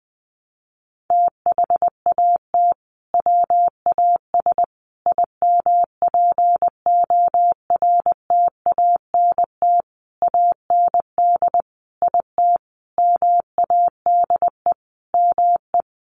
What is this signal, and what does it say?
Morse code, heard as a keyed tone, about 20 wpm, 700 Hz. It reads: THAT WAS IMPORTANT AND IT MADE ME
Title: Sentences from Top 600 Words 20wpm